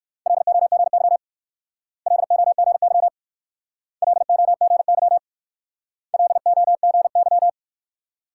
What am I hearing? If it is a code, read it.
Morse code, heard as a keyed tone, about 55 wpm, 700 Hz. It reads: LOGY LOGY LOGY LOGY
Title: Speed-Racing Multi-letter Phonemes - 45wpm